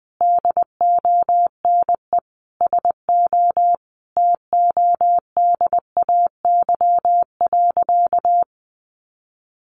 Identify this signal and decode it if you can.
Morse code, heard as a keyed tone, about 20 wpm, 700 Hz. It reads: DONE SO TODAY.
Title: Sentences from Top 1500 Words 20wpm